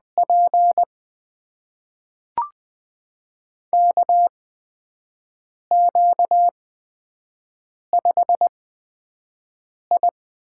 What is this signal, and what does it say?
Morse code, heard as a keyed tone, about 20 wpm, spaced as for 5 wpm, 700 Hz. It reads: P KQ5I
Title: USA Call Signs - 4 Characters - 8x Character Spacing - 20wpm